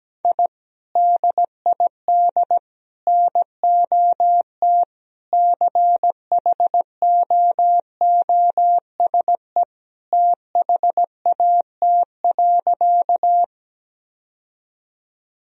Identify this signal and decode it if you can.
Morse code, heard as a keyed tone, about 17 wpm, 700 Hz. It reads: I DID NOT CHOOSE THAT.
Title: Sentences from Top 1000 Words - Review Part 4 - 17wpm